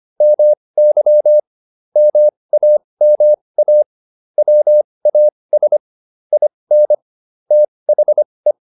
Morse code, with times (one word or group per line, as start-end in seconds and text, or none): 0.2-1.4 MY
2.0-3.8 MAMA
4.4-5.8 WAS
6.3-6.9 IN
7.5-8.5 THE